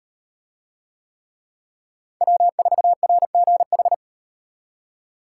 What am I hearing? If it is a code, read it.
Morse code, heard as a keyed tone, about 35 wpm, 700 Hz. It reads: W4RGH